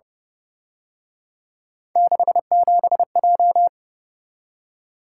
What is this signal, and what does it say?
Morse code, heard as a keyed tone, about 31 wpm, 700 Hz. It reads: 67J
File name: Speed-Racing - Three Letter-Number Combinations - 25wpm